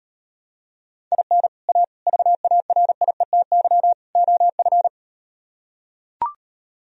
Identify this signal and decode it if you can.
Morse code, heard as a keyed tone, about 38 wpm, 700 Hz. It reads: IN A VARIETY OF